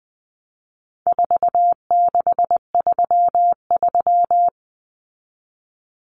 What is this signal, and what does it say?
Morse code, heard as a keyed tone, about 20 wpm, 700 Hz. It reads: 4633
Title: Common Cut Numbers - 4-Digit 20wpm